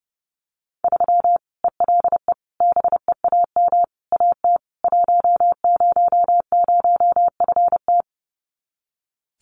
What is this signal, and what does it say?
Morse code, heard as a keyed tone, about 28 wpm, 700 Hz. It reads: 3 ELE BEAM AT 100FT
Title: Common QSO Phrases 30wpm